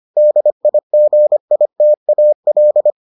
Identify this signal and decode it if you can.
Morse code, heard as a keyed tone, about 25 wpm, 600 Hz. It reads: DIGITAL